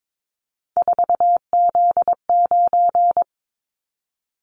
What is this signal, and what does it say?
Morse code, heard as a keyed tone, about 22 wpm, 700 Hz. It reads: 4Z9
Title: Callsign Prefixes for Top 50 DXCC - 22wpm